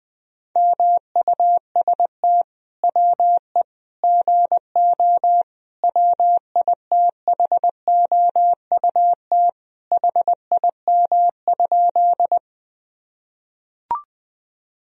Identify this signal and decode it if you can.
Morse code, heard as a keyed tone, about 20 wpm, 700 Hz. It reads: MUST WE GO WITHOUT HIM?
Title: Sentences from Top 1000 Words - Review Part 3 - 20wpm